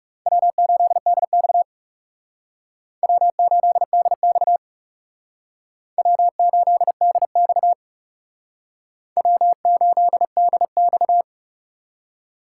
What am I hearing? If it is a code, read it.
Morse code, heard as a keyed tone, about 36 wpm, 700 Hz. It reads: W8DX W8DX W8DX W8DX
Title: Speed Racing - USA 1x2 Call Signs 30wpm